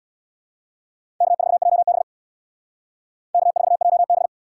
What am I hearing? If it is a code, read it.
Morse code, heard as a keyed tone, about 77 wpm, 700 Hz. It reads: K4JB K4JB